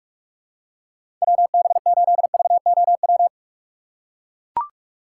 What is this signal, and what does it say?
Morse code, heard as a keyed tone, about 41 wpm, 700 Hz. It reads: WB8VOW
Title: USA Call Signs - 6 Characters - 6x Character Spacing - 45wpm